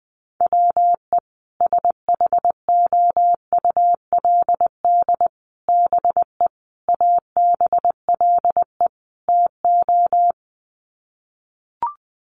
Morse code, heard as a keyed tone, about 20 wpm, 700 Hz. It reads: WE SHOULD BE ABLE TO
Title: Sets of 5 Words Encore2 20wpm